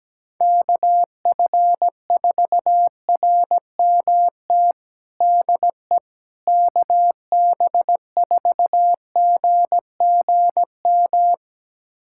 Morse code, heard as a keyed tone, about 17 wpm, 700 Hz. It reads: KF4RMT DE KB4GGM
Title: USA Callsign DE Callsign - 6 Characters - 17wpm